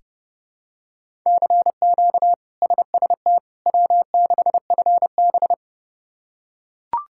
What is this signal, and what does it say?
Morse code, heard as a keyed tone, about 30 wpm, 700 Hz. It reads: CQ SST W6FB
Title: K1USN SST QSO Practice - 30wpm